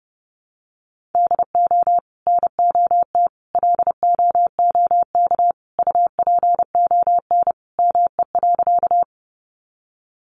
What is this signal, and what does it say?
Morse code, heard as a keyed tone, about 30 wpm, 700 Hz. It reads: DO NOT LOOK UPON ME.
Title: Sentences from Top 1100 Words 30wpm